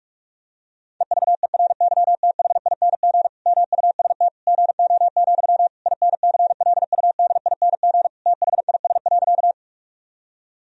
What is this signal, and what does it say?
Morse code, heard as a keyed tone, about 46 wpm, 700 Hz. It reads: EVERYTHING MUST GO, INCLUDING THIS.
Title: Sentences from Top 1000 Words - Review Part 4 - 45wpm